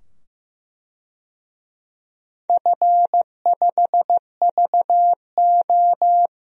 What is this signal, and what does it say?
Morse code, heard as a keyed tone, about 15 wpm, 700 Hz. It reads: F5VO
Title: DXCC-100 Call Signs - 4 Characters - 15wpm